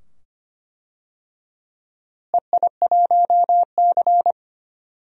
Morse code, heard as a keyed tone, about 25 wpm, 700 Hz. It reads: EI1C
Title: DXCC-100 Call Signs - 4 Characters - Repeated 2x - 25wpm